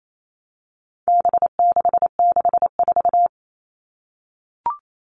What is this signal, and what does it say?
Morse code, heard as a keyed tone, about 28 wpm, 700 Hz. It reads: B664
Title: All Cut Numbers - 4-Digit 28wpm